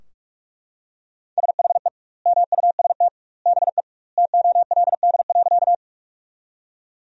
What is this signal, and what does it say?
Morse code, heard as a keyed tone, about 44 wpm, 700 Hz. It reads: SHE MUST BE TOLD.
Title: 2x Word-Spacing - Sentences from Top 400 Words - 45wpm